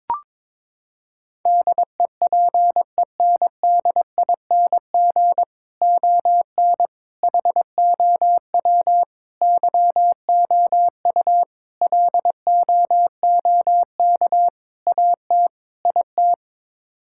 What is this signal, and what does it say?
Morse code, heard as a keyed tone, about 22 wpm, 700 Hz. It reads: DEPENDING ON HOW YOU LOOK AT IT
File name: practice / 22wpm / Sets of 7 Words - 22wpm